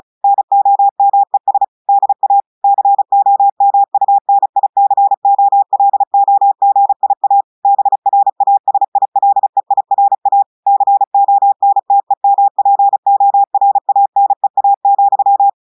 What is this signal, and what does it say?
Morse code, heard as a keyed tone, about 36 wpm, 800 Hz. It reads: NOMES DA COMUNICOLOGIA BRASILEIRA CONTEMPORANEA,
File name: audios / portugais / portugais texte en clair 1 vitesse 35 mots minutes